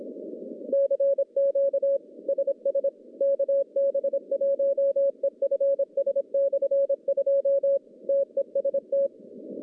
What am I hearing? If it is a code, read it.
Morse code, heard as a keyed tone, about 26 wpm, 550 Hz. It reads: CQ SS KB1EFS/2 TEST
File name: contests / CW-SS-k5tr- -1100